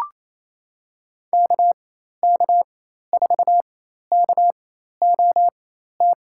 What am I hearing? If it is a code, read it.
Morse code, heard as a keyed tone, about 28 wpm, spaced as for 13 wpm, 700 Hz. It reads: KK4KOT
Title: USA Call Signs - 6 Characters - 4x Character Spacing - 28wpm